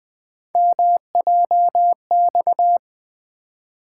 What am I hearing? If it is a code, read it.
Morse code, heard as a keyed tone, about 20 wpm, 700 Hz. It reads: MJX
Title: Three Letter-Number Combinations - 20wpm